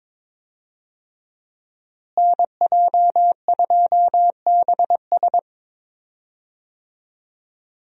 Morse code, heard as a keyed tone, about 22 wpm, 700 Hz. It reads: NJ2BS
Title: USA Call Signs - 5 Characters - Repeated 3x - 22wpm